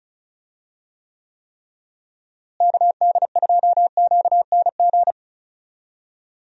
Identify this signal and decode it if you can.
Morse code, heard as a keyed tone, about 34 wpm, 700 Hz. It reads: KD2QNG
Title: USA Call Signs - 6 Characters - Repeated 3x - 35wpm